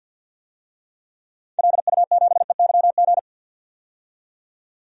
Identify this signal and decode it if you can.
Morse code, heard as a keyed tone, about 53 wpm, 700 Hz. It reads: LU7EYG
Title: DXCC-100 Call Signs - 6 Characters - 50wpm